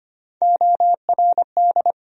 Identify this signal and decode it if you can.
Morse code, heard as a keyed tone, about 25 wpm, 700 Hz. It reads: ORD
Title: USA Call Sign Postfixes - 3 Letters 25wpm